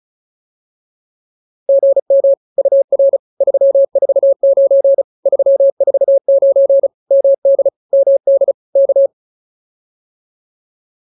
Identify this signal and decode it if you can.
Morse code, heard as a keyed tone, about 34 wpm, 550 Hz. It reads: GM UR 349 349 MD MD K
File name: POTA302 35